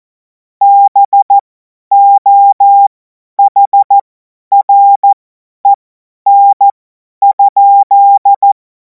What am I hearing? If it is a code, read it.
Morse code, heard as a keyed tone, about 14 wpm, spaced as for 10 wpm, 800 Hz. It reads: BOHREN?
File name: Morsekurs / Witze / Witz 31. März 2026 50 F70